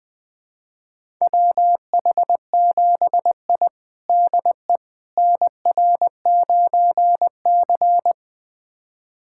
Morse code, heard as a keyed tone, about 21 wpm, 700 Hz. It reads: WH7I DE NR9C
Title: USA Callsign DE Callsign - 4 Characters - 20wpm